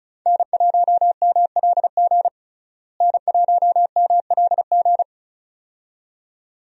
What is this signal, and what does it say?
Morse code, heard as a keyed tone, about 36 wpm, 700 Hz. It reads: N1MLG N1MLG
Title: USA Call Signs - 5 Characters - Repeated 2x - 35wpm